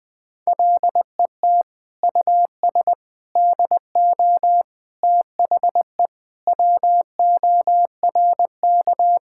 Morse code, heard as a keyed tone, about 20 wpm, 700 Hz. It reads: LET US DO THE WORK
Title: Sentences from Top 1600 Words 20wpm